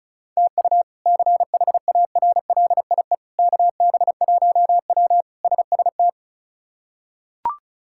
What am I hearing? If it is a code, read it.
Morse code, heard as a keyed tone, about 36 wpm, 700 Hz. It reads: TU CHARLIE KB1W SST
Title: K1USN SST QSO Practice - 35wpm